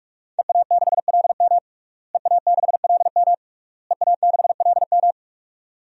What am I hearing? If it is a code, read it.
Morse code, heard as a keyed tone, about 46 wpm, 700 Hz. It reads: EA6LM EA6LM EA6LM
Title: DXCC-100 Call Signs - 5 Characters - Repeated 3x - 45wpm